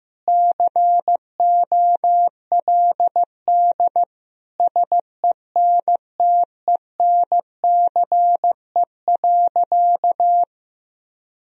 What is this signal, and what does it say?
Morse code, heard as a keyed tone, about 15 wpm, 700 Hz. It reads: COLD SENTENCE.
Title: Sentences from Top 1000 Words - Review Part 1 - 15wpm